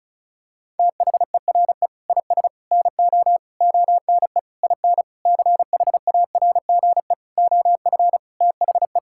Morse code, heard as a keyed tone, about 36 wpm, 690 Hz. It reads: THERE IS NO ONE IN CHARGE OF THE
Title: Sentences from Top 1000 Words - Review Part 6 - 35wpm